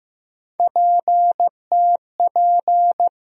Morse code, heard as a keyed tone, about 15 wpm, 700 Hz. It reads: PTP
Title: USA Call Sign Postfixes - 3 Letters 15wpm